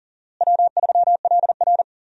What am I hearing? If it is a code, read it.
Morse code, heard as a keyed tone, about 40 wpm, 700 Hz. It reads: W3LR
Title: USA Call Signs - 4 Characters - Repeated 3x - 40wpm